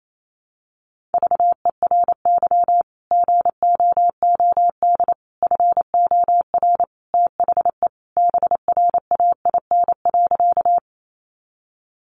Morse code, heard as a keyed tone, about 28 wpm, 700 Hz. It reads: VERY GOOD FOR THE BRAIN.
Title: Sentences from Top 1100 Words 28wpm